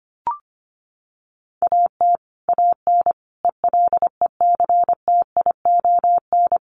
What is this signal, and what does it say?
Morse code, heard as a keyed tone, about 25 wpm, 700 Hz. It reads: AT AN ELECTION
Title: Sentences from Top 1500 Words 25wpm